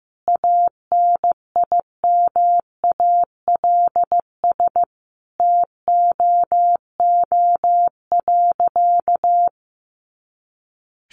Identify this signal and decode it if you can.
Morse code, heard as a keyed tone, about 14 wpm, 700 Hz. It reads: ANIMALS TOO.
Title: Easy Phrases 15wpm